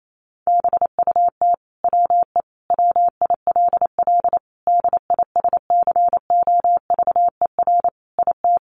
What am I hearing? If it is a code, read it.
Morse code, heard as a keyed tone, about 28 wpm, 700 Hz. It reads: BUT WE WILL DISCOVER IT
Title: Sentences from Top 1100 Words 28wpm